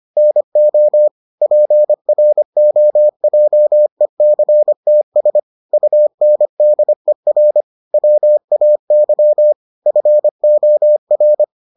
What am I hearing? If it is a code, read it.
Morse code, heard as a keyed tone, about 25 wpm, 600 Hz. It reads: NO PROJECTS UNDER WAY FOR